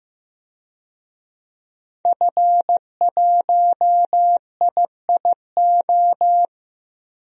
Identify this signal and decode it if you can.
Morse code, heard as a keyed tone, about 15 wpm, 700 Hz. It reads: F1IIO
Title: DXCC-100 Call Signs - 5 Characters - 15wpm